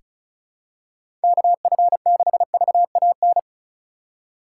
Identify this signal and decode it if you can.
Morse code, heard as a keyed tone, about 36 wpm, 700 Hz. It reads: KF6VAN
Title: USA Call Signs - 6 Characters - 4x Character Spacing - 35wpm